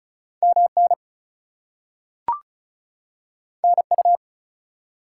Morse code, heard as a keyed tone, about 35 wpm, 700 Hz. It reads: MN NU
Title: US States and Canadian Provinces 35wpm